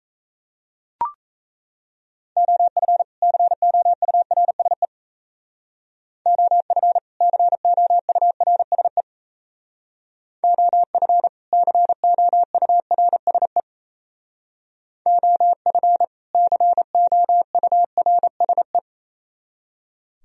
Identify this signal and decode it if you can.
Morse code, heard as a keyed tone, about 35 wpm, 690 Hz. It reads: OF COURSE OF COURSE OF COURSE OF COURSE
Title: Speed-Racing - Sets of 2 Words 28wpm